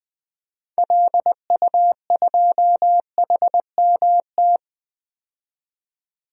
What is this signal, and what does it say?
Morse code, heard as a keyed tone, about 20 wpm, 700 Hz. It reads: LU2HMT
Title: DXCC-100 Call Signs - 6 Characters - 20wpm